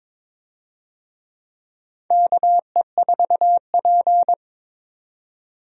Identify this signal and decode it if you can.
Morse code, heard as a keyed tone, about 22 wpm, 700 Hz. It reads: KE4P
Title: USA Call Signs - 4 Characters - 22wpm